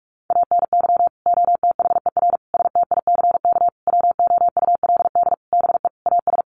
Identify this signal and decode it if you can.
Morse code, heard as a keyed tone, about 44 wpm, 700 Hz. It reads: ANY OTHER STICK WOULD BE AS
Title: Sentences from Top 1200 Words 45wpm